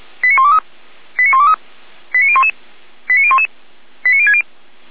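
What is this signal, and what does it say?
Five-tone selective call (ZVEI), heard as quick runs of five tones, tones of about 70 ms. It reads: 89123 89123 89011 89011 89060
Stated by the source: ZVEI five-tone selective call received via SDR, recorded by Feuerwehr Orlamünde.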